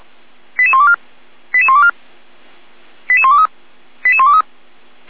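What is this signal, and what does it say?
Five-tone selective call (ZVEI), heard as quick runs of five tones, tones of about 70 ms. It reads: 88124 88124 88123 88123
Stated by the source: Received by SDR; ZVEI five-tone selective call.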